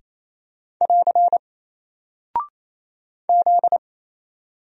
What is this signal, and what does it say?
Morse code, heard as a keyed tone, about 27 wpm, 700 Hz. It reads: <AR> Z
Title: Single Character (Letter:Number:Punctuation:Prosign) 28wpm